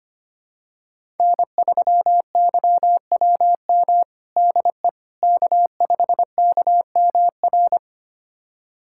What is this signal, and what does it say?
Morse code, heard as a keyed tone, about 25 wpm, 700 Hz. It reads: N3YWM DE K5KMR
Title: USA Callsign DE Callsign - 5 Characters - 25wpm